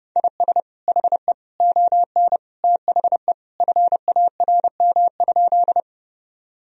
Morse code, heard as a keyed tone, about 30 wpm, 700 Hz. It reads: IS HE ON THE FARM?
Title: Sentences from Top 1000 Words - Review Part 2 - 30wpm